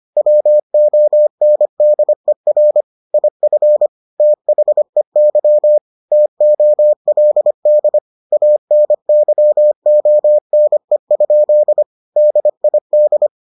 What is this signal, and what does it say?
Morse code, heard as a keyed tone, about 25 wpm, 600 Hz. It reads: WONDER IF THEY TOLD ANYONE? DID